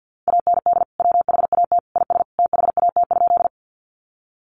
Morse code, heard as a keyed tone, about 50 wpm, 700 Hz. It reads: AND WHAT IS THAT?
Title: Sentences from Top 100 Words 50wpm